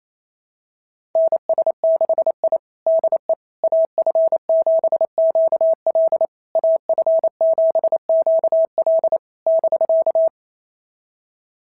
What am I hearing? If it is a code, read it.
Morse code, heard as a keyed tone, about 27 wpm, 660 Hz. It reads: NS6I DE AF7QL AF7QL <BK>
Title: SOTA QSO Practice 28wpm